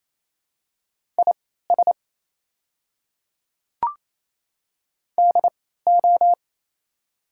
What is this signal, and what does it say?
Morse code, heard as a keyed tone, about 28 wpm, spaced as for 16 wpm, 700 Hz. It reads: IS DO
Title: 3x Character Spacing - Top 200 Words - 28wpm